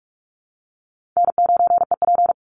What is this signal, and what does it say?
Morse code, heard as a keyed tone, about 44 wpm, 700 Hz. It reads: N9EP
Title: Easy Call Signs Repeated 2x - 45wpm